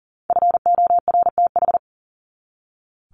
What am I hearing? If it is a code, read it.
Morse code, heard as a keyed tone, about 40 wpm, 700 Hz. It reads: FORTH
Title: Top 1751-2000 Words 40wpm